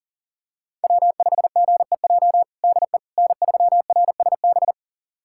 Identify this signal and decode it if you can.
Morse code, heard as a keyed tone, about 40 wpm, 700 Hz. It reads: W5GEJ DE N3RSB
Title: USA Callsign DE Callsign - 5 Characters - 40wpm